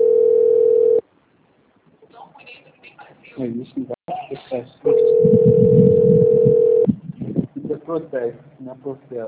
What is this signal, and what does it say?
Ringback tone (North America), 440 + 480 Hz, with speech.
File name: out- - - - - .69239